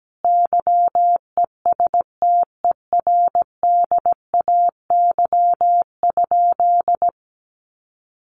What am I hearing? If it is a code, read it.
Morse code, heard as a keyed tone, about 17 wpm, 700 Hz. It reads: YESTERDAY?
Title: A Sound Of Thunder 17wpm